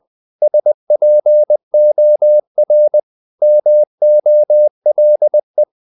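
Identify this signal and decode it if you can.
Morse code, heard as a keyed tone, about 20 wpm, 600 Hz. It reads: SPOR MOLE
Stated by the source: Morse code practice of short Danish words at 20 wpm, from CW med Gnister.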